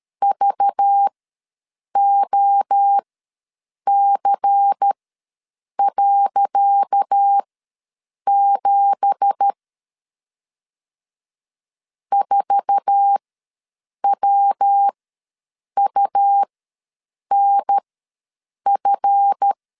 Morse code, heard as a keyed tone, about 13 wpm, spaced as for 7 wpm, 800 Hz. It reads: VOC.7 4WUNF
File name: LEZIONI / 10 SIGNS / 10C DOT ws 12wpm 3548